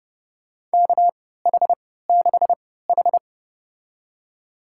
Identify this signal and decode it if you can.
Morse code, heard as a keyed tone, about 30 wpm, spaced as for 17 wpm, 700 Hz. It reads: KH6H
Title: USA Call Signs - 3x Character Spacing - 30wpm